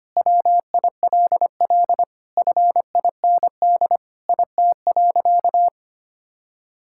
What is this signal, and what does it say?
Morse code, heard as a keyed tone, about 25 wpm, 700 Hz. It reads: WILL FIND IT.
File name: practice / 25wpm / Sentences with 3-Word Phrases - 25wpm